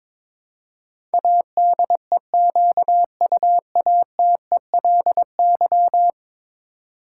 Morse code, heard as a keyed tone, about 22 wpm, 700 Hz. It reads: ADEQUATELY
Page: morse-code-ninja-practice-22wpm